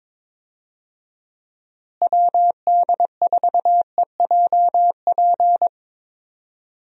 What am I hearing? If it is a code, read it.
Morse code, heard as a keyed tone, about 22 wpm, 700 Hz. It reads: WD4EJP
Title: USA Call Signs - 6 Characters - Repeated 3x - 22wpm